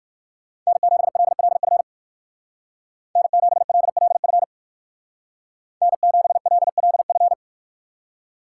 Morse code, heard as a keyed tone, about 55 wpm, 700 Hz. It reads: N7LLF N7LLF N7LLF
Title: Speed Racing - USA 1x3 Call Signs 45wpm